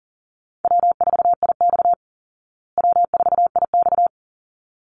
Morse code, heard as a keyed tone, about 40 wpm, 700 Hz. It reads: W4IX W4IX
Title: Easy Call Signs Repeated 2x - 40wpm